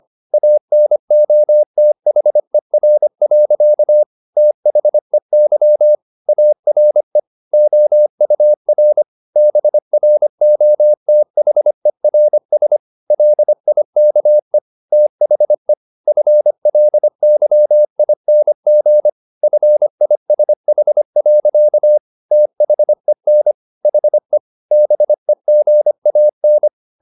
Morse code, heard as a keyed tone, about 25 wpm, 600 Hz. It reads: ANOTHER. THEY ARE OUR BROTHERS LIKE THE FLYING FISH. THEN HE BEGAN
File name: the old man and the sea part 1 25wpm